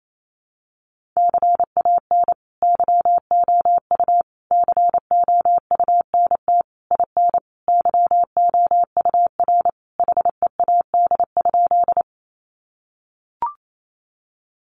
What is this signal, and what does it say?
Morse code, heard as a keyed tone, about 28 wpm, 700 Hz. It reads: CAN YOU COUNT IN YOUR HEAD?
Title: Sentences from Top 1500 Words 28wpm